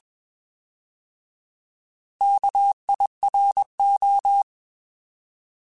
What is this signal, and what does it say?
Morse code, heard as a keyed tone, about 21 wpm, 800 Hz. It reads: KIRO
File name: Alphabetical / Alphabetical mp3s / Ks @20